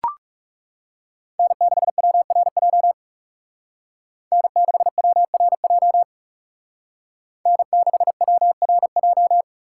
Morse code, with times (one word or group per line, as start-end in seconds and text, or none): 1.4-2.9 N6WRJ
4.3-6.0 N6WRJ
7.5-9.4 N6WRJ